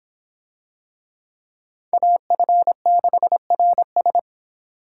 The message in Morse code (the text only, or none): AF6RS